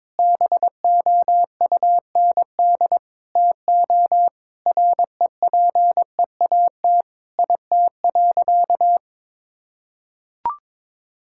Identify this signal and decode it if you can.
Morse code, heard as a keyed tone, about 22 wpm, 700 Hz. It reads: BOUND TO REPEAT IT.
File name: practice / 22wpm / Sentences with 8-Word Phrases - 22wpm